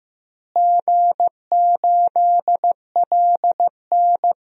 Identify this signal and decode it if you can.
Morse code, heard as a keyed tone, about 15 wpm, 700 Hz. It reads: G8LN
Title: DXCC-100 Call Signs - 4 Characters - Repeated 2x - 15wpm